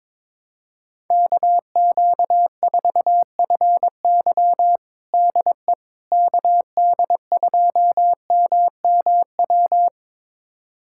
Morse code, heard as a keyed tone, about 21 wpm, 700 Hz. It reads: KQ4FY DE KD2MMW
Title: USA Callsign DE Callsign 22wpm